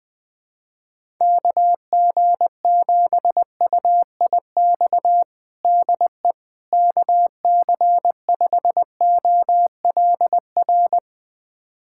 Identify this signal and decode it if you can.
Morse code, heard as a keyed tone, about 20 wpm, 700 Hz. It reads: KG7UIX DE KC5OLR
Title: USA Callsign DE Callsign - 6 Characters - 20wpm